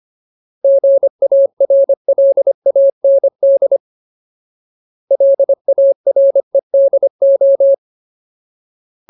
Morse code, heard as a keyed tone, about 25 wpm, 550 Hz. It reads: GARLAND LAREDO
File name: Ultimate Word List / 25WPM Ultimate